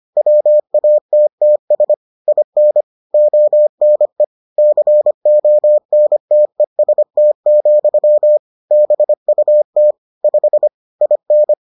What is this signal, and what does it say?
Morse code, heard as a keyed tone, about 25 wpm, 600 Hz. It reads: WATTS IN ONE CONTEST, BUT 5 IN